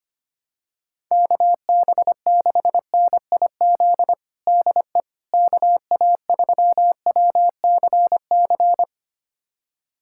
Morse code, heard as a keyed tone, about 25 wpm, 700 Hz. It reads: KB6NIZ DE KA3WCC
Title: USA Callsign DE Callsign - 6 Characters - 25wpm